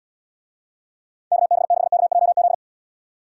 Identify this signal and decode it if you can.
Morse code, heard as a keyed tone, about 79 wpm, 700 Hz. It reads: KB6KJZ